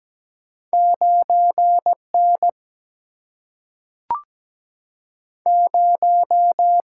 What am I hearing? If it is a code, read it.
Morse code, heard as a keyed tone, about 17 wpm, 700 Hz. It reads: 9N 0
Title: Single-Letter Single-Number 17wpm